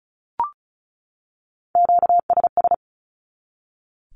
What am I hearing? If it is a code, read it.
Morse code, heard as a keyed tone, about 34 wpm, 700 Hz. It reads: QSS
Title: Q-Codes 35wpm